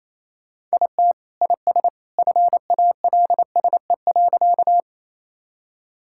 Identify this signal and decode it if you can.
Morse code, heard as a keyed tone, about 28 wpm, 700 Hz. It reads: IT IS FALSE.
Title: Sentences from Top 1600 Words 28wpm